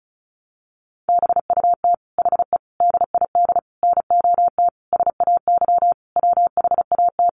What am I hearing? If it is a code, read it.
Morse code, heard as a keyed tone, about 34 wpm, 700 Hz. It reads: BUT HE DID NOT SAY WHAT
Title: Sentences from Top 600 Words 35wpm